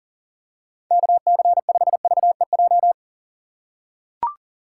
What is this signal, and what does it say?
Morse code, heard as a keyed tone, about 37 wpm, 700 Hz. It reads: KC5VEJ
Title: USA Call Signs - 6 Characters - 4x Character Spacing - 40wpm